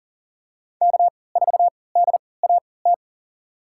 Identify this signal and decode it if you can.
Morse code, heard as a keyed tone, about 40 wpm, spaced as for 23 wpm, 700 Hz. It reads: K4DAT
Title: USA Call Signs - 3x Character Spacing - 40wpm